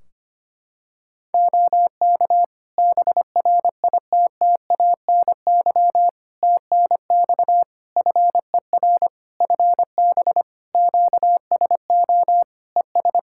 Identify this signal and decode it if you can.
Morse code, heard as a keyed tone, about 25 wpm, 700 Hz. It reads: OK BRITTANY TNX FER FB QSO ES
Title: Practice QSO - Exchange 4 - Ending - 25wpm